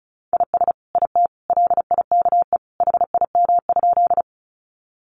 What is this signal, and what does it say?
Morse code, heard as a keyed tone, about 34 wpm, 700 Hz. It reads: IS IT LIKE HIM?
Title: Sentences from Top 100 Words 35wpm